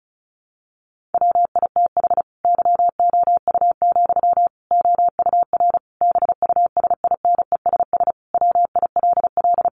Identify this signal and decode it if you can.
Morse code, heard as a keyed tone, about 34 wpm, 700 Hz. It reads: WITH YOU, OUR BUSINESS WILL
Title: Sentences from Top 1400 Words 35wpm